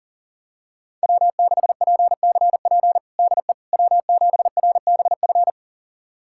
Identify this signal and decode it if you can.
Morse code, heard as a keyed tone, about 37 wpm, 700 Hz. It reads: W6PCP DE W7RBF
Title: USA Callsign DE Callsign - 5 Characters - 40wpm